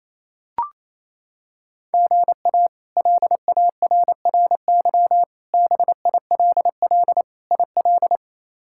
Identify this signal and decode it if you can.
Morse code, heard as a keyed tone, about 28 wpm, 700 Hz. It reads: GA LARRY BILL IL
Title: K1USN SST QSO Practice - 28wpm